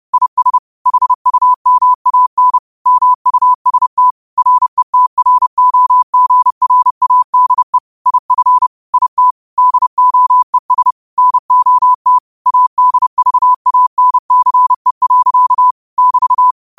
Morse code, heard as a keyed tone, about 30 wpm, 1000 Hz. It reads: IS HUMAN MUST RETROGRADE IF IT DOES NOT ADVANCE. <BT>
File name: qotdfiles / 30wpm